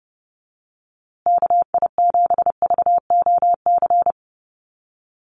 Morse code, heard as a keyed tone, about 30 wpm, 700 Hz. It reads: KI7VOC
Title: USA Call Signs Repeated 2x 30wpm